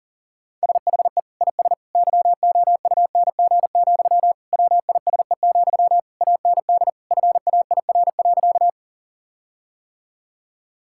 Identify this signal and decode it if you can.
Morse code, heard as a keyed tone, about 40 wpm, 700 Hz. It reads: SHE IS YOUNG, WISE, AND FAIR.
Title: Sentences from Top 1600 Words 40wpm